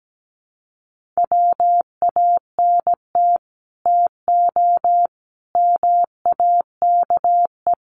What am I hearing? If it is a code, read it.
Morse code, heard as a keyed tone, about 17 wpm, 700 Hz. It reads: WANT TO MAKE
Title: Sets of 5 Words 17wpm